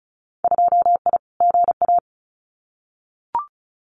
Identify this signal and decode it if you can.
Morse code, heard as a keyed tone, about 36 wpm, 700 Hz. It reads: 2I GA
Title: Winter Field Day Exchanges 35wpm